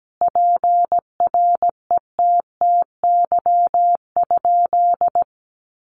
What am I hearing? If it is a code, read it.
Morse code, heard as a keyed tone, about 17 wpm, 700 Hz. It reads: PRETTY?
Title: Sentences from Top 1000 Words 17wpm